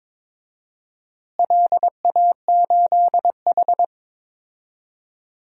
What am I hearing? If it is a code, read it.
Morse code, heard as a keyed tone, about 21 wpm, 700 Hz. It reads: LA8H
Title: DXCC-100 Call Signs - 4 Characters - Repeated 2x - 22wpm